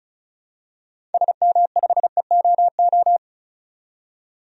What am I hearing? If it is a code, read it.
Morse code, heard as a keyed tone, about 36 wpm, 700 Hz. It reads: SM5EOO
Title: DXCC-100 Call Signs - 6 Characters - 35wpm